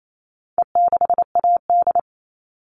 Morse code, heard as a keyed tone, about 28 wpm, 700 Hz. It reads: E6AD
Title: All Cut Numbers - 4-Digit 28wpm